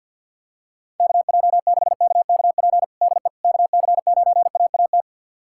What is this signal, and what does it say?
Morse code, heard as a keyed tone, about 48 wpm, 700 Hz. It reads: KJ6KKP DE KC9AAT
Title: USA Callsign DE Callsign - 6 Characters - 50wpm